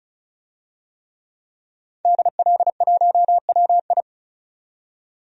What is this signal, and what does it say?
Morse code, heard as a keyed tone, about 36 wpm, 700 Hz. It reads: DL1WI
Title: DXCC-100 Call Signs - 5 Characters - 35wpm